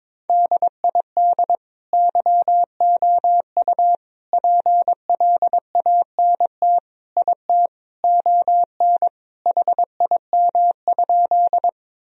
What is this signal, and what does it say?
Morse code, heard as a keyed tone, about 22 wpm, 700 Hz. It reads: DID YOU PLANT IT ON HIM?
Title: Sentences from Top 1000 Words - Review Part 1 - 22wpm